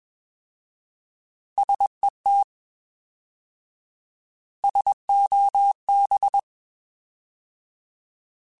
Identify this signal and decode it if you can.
Morse code, heard as a keyed tone, about 21 wpm, 800 Hz. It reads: SET SOB
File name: Alphabetical / Alphabetical mp3s / Ss @20